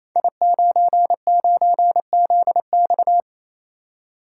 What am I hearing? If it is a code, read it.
Morse code, heard as a keyed tone, about 27 wpm, 700 Hz. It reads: I99ZX
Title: Code Groups - 5 Letter:Numbers - 28wpm